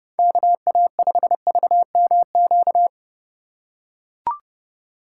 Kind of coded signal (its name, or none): Morse code